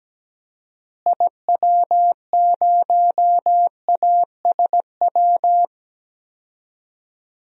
Morse code, heard as a keyed tone, about 17 wpm, 700 Hz. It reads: IW0ASW
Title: DXCC-100 Call Signs - 6 Characters - Repeated 2x - 17wpm